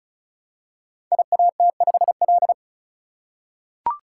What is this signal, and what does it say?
Morse code, heard as a keyed tone, about 36 wpm, 700 Hz. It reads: IAT5L
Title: Code Groups - 5 Letter:Numbers - 35wpm